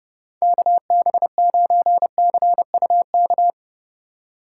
Morse code, heard as a keyed tone, about 32 wpm, 700 Hz. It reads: KB9CUK
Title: USA Call Signs - 6 Characters - 3x Character Spacing - 30wpm